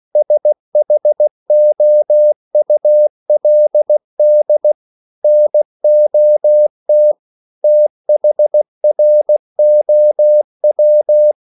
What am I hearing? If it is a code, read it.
Morse code, heard as a keyed tone, about 16 wpm, 600 Hz. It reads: SHOULD NOT THROW
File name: CW / Practice / EnglishSayings16